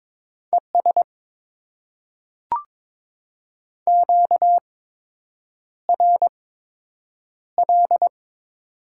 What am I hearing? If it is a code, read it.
Morse code, heard as a keyed tone, about 22 wpm, spaced as for 8 wpm, 700 Hz. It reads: ES QRL
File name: Top 20 QSO Elements - 8x Character Spacing - 22wpm